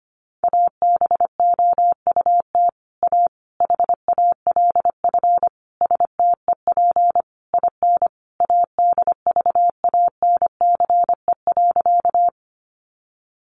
Morse code, heard as a keyed tone, about 25 wpm, 700 Hz. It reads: ABOUT A HALF STEP IN ADVANCE.
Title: Sentences from Top 1400 Words 25wpm